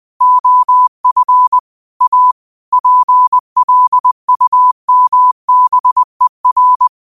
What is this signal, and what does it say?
Morse code, heard as a keyed tone, about 20 wpm, 1000 Hz. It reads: OF A PLUMBER
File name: qotdfiles / 20wpm